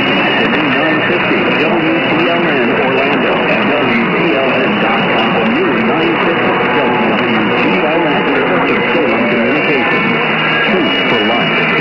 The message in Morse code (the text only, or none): none